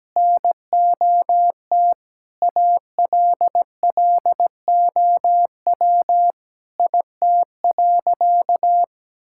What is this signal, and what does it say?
Morse code, heard as a keyed tone, about 17 wpm, 700 Hz. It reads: NOT ALLOW IT.
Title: Sentences from Top 1000 Words - Review Part 5 - 17wpm